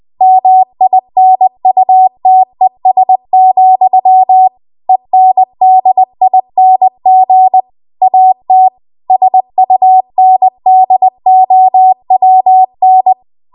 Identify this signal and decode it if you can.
Morse code, heard as a keyed tone, about 20 wpm, 750 Hz. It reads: MINUTES, ENDING AT SUNDOWN
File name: Archive / 20 WPM / 20WPM